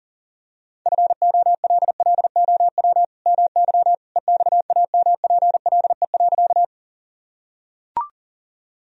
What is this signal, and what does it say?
Morse code, heard as a keyed tone, about 40 wpm, 700 Hz. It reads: FOLLOW MY EXAMPLE.
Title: Sentences from Top 1000 Words - Review Part 1 - 40wpm